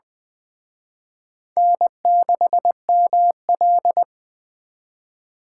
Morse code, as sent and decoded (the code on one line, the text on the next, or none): -. -.... -- .-..
N6ML